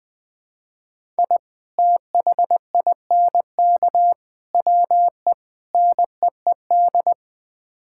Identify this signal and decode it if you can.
Morse code, heard as a keyed tone, about 20 wpm, 700 Hz. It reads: I THINK WE NEED
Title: Build A Phrase - 5-Word Phrases - 20wpm